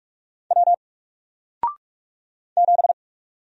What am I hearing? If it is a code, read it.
Morse code, heard as a keyed tone, about 46 wpm, 700 Hz. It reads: W 7
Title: Single Character (Letter:Number:Punctuation:Prosign) Alt 45wpm